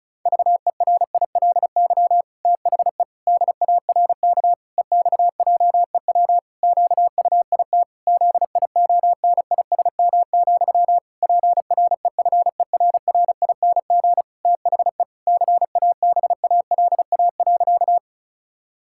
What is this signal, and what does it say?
Morse code, heard as a keyed tone, about 36 wpm, 700 Hz. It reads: VERILY THE DARK EXJEW QUIT ZIONISM, PREFERRING THE CABALA.
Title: Pangram Challenge v2 - 35wpm